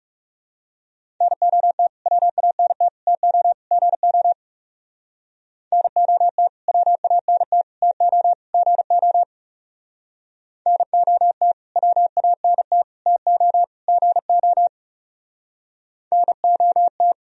Morse code, heard as a keyed tone, about 38 wpm, 700 Hz. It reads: NOT WANT TO GO NOT WANT TO GO NOT WANT TO GO NOT